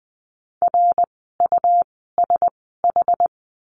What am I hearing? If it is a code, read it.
Morse code, heard as a keyed tone, about 20 wpm, spaced as for 14 wpm, 700 Hz. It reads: RUSH
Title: 101 Lesson 17f - Letter u - Words 20wpm